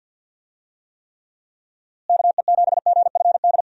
Morse code, heard as a keyed tone, about 49 wpm, 700 Hz. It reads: KE7GFD
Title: USA Call Signs - 6 Characters - Repeated 3x - 50wpm